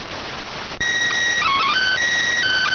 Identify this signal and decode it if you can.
SSTV (slow-scan television), PD 290.